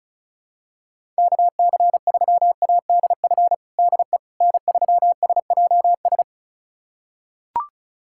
Morse code, heard as a keyed tone, about 34 wpm, 700 Hz. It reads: KC3ADF DE N3SJS
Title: USA Callsign DE Callsign 35wpm